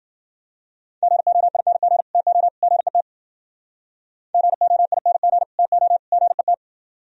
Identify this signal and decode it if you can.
Morse code, heard as a keyed tone, about 48 wpm, 700 Hz. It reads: GOING TO GET GOING TO GET
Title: Speed-Racing - Sets of 3 Words 40wpm